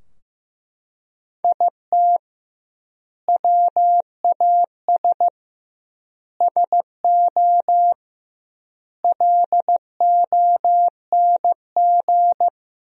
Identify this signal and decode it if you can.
Morse code, heard as a keyed tone, about 15 wpm, 700 Hz. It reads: IT WAS SO LONG
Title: 2x Word-Spacing - Sentences from Top 500 Words - 15wpm